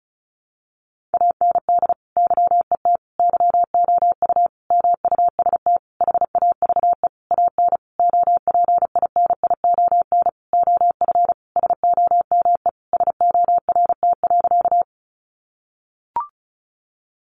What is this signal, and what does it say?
Morse code, heard as a keyed tone, about 34 wpm, 700 Hz. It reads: AND YET YOU MUST HAVE AN OPINION OF SOME SORT.
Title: Sentences from Top 1300 Words 35wpm